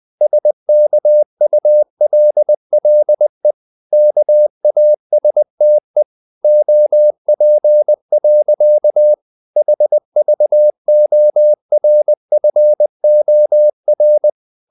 Morse code, heard as a keyed tone, about 20 wpm, 600 Hz. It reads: SKULLE KASTE OP. HVORFOR
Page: Ferie 20 wpm